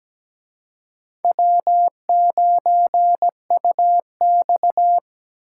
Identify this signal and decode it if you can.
Morse code, heard as a keyed tone, about 20 wpm, 700 Hz. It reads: W9UX
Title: Speed Racing - USA 1x2 Call Signs 17wpm